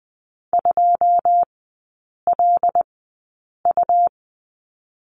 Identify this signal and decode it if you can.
Morse code, heard as a keyed tone, about 20 wpm, spaced as for 8 wpm, 700 Hz. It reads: 2LU
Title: 100 Lesson 17e - Letter u - Three Characters 20wpm